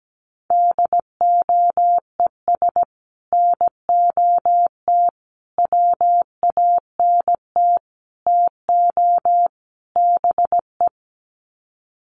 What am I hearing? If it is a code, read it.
Morse code, heard as a keyed tone, about 17 wpm, 700 Hz. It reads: DOES NOT WANT TO BE
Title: Sets of 5 Words Encore 17wpm